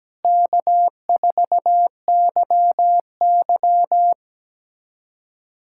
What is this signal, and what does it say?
Morse code, heard as a keyed tone, about 17 wpm, 700 Hz. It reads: K4YY
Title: USA Callsign DE Callsign - 4 Characters - 17wpm